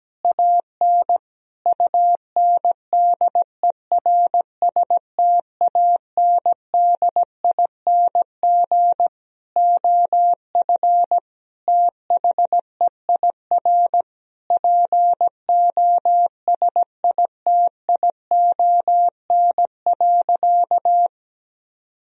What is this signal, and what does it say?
Morse code, heard as a keyed tone, about 17 wpm, 700 Hz. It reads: AN UNDERSTANDING OF THEIR POSITION.
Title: Sentences from Top 1000 Words - Review Part 3 - 17wpm